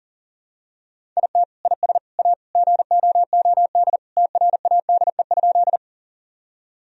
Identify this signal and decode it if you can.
Morse code, heard as a keyed tone, about 40 wpm, 700 Hz. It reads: IT IS A GOOD TRADE?
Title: Sentences from Top 1000 Words - Review Part 4 - 40wpm